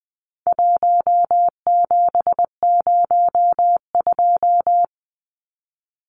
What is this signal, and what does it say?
Morse code, heard as a keyed tone, about 20 wpm, 700 Hz. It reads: 1702
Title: Numbers - 4-Digit 20wpm